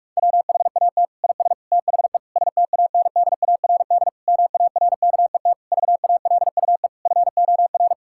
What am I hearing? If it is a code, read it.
Morse code, heard as a keyed tone, about 47 wpm, 700 Hz. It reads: WHAT IS THE STANDARD MARKET VALUE FOR
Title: Sentences from Top 1000 Words - Review Part 6 - 45wpm